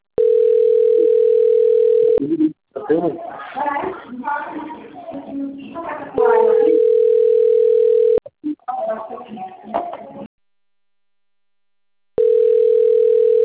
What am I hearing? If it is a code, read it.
Ringback tone (North America), 440 + 480 Hz, with speech.
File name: rg-1600-unknown- - - .44086